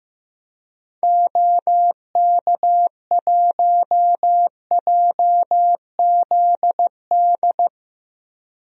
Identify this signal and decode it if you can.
Morse code, heard as a keyed tone, about 15 wpm, 700 Hz. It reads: OK1JZD
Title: DXCC-100 Call Signs - 6 Characters - Repeated 2x - 15wpm